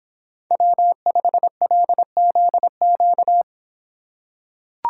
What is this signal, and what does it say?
Morse code, heard as a keyed tone, about 26 wpm, 700 Hz. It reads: W5LZQ